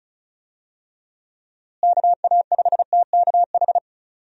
Morse code, heard as a keyed tone, about 36 wpm, 700 Hz. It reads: KA5TKH
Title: USA Call Signs - 6 Characters - Repeated 2x - 35wpm